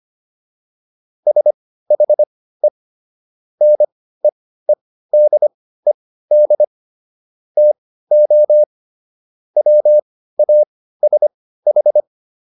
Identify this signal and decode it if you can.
Morse code, heard as a keyed tone, about 25 wpm, spaced as for 15 wpm, 600 Hz. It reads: SHE NEEDED TO WASH